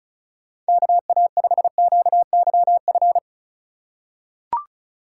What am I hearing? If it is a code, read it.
Morse code, heard as a keyed tone, about 36 wpm, 700 Hz. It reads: KA5QYF
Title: USA Call Signs - 6 Characters - 4x Character Spacing - 35wpm